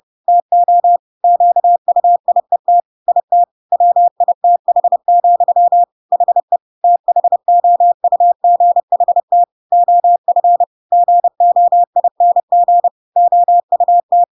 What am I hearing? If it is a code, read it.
Morse code, heard as a keyed tone, about 30 wpm, 700 Hz. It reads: TO QUIET IT WITH, HE THOUGHT OF GOING OUT
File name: Pinocchio / Pinocchio ch5 30 30 ik1hge